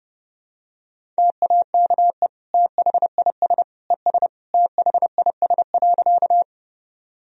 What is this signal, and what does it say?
Morse code, heard as a keyed tone, about 30 wpm, 700 Hz. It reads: TAKE THIS ES THIS.
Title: Sentences from Top 200 Words and Abbreviations 30wpm